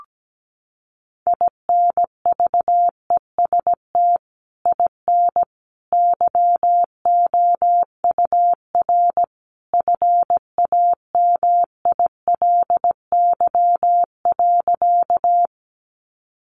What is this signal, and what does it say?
Morse code, heard as a keyed tone, about 17 wpm, 700 Hz. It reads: INVEST IN YOUR FAMILY.
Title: Sentences from Top 1100 Words 17wpm